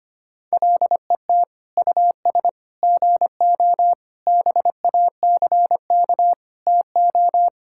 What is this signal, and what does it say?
Morse code, heard as a keyed tone, about 25 wpm, 700 Hz. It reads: LET US GO BACK TO
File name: practice / 25wpm / Sentences with 4-Word Phrases - 25wpm